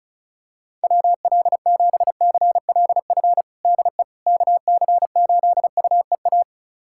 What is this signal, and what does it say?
Morse code, heard as a keyed tone, about 36 wpm, 690 Hz. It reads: WL7CLF DE KC8UEA
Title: USA Callsign DE Callsign - 6 Characters - 35wpm